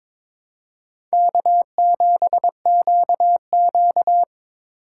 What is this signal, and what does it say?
Morse code, heard as a keyed tone, about 22 wpm, 700 Hz. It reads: K7QQ
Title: USA Call Signs - 4 Characters - 3x Character Spacing - 22wpm